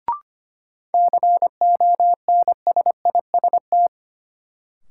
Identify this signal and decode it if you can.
Morse code, heard as a keyed tone, about 25 wpm, 700 Hz. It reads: CONSIST
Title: Sentences from Top 1400 Words - New Word Drill - 25wpm